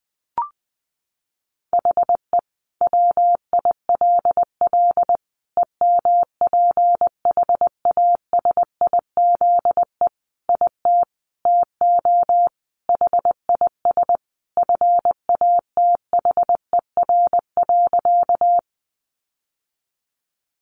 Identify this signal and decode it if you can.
Morse code, heard as a keyed tone, about 20 wpm, 700 Hz. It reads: HE WILL EMPHASIZE IT TO HIS FATHER.
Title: Sentences from Top 1500 Words 20wpm